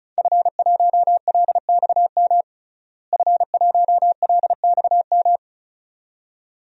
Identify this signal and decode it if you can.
Morse code, heard as a keyed tone, about 36 wpm, 700 Hz. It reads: F1LXM F1LXM
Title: DXCC-100 Call Signs - 5 Characters - Repeated 2x - 35wpm